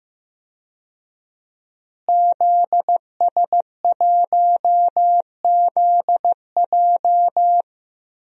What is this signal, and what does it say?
Morse code, heard as a keyed tone, about 15 wpm, 700 Hz. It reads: ZS1ZJ
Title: DXCC-100 Call Signs - 5 Characters - Repeated 3x - 15wpm